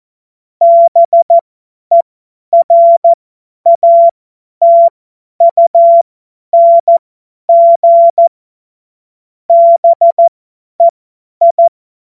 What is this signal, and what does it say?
Morse code, heard as a keyed tone, about 14 wpm, spaced as for 10 wpm, 680 Hz. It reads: BERATUNG BEI
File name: Morsekurs / Witze / Witz 23. August 2022 50 F70